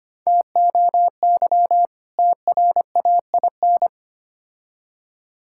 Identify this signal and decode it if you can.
Morse code, heard as a keyed tone, about 25 wpm, 700 Hz. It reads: TOY TRAIN
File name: practice / 25wpm / Difficult Words with Hints - 25wpm